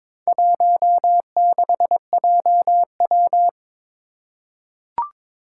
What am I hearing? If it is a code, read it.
Morse code, heard as a keyed tone, about 22 wpm, 700 Hz. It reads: 16JW
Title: Four Challenging Letter-Number Combinations 22wpm